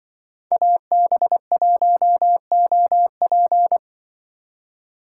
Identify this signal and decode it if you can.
Morse code, heard as a keyed tone, about 25 wpm, 700 Hz. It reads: AB1OP
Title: Speed Racing - USA 2x2 Call Signs 20wpm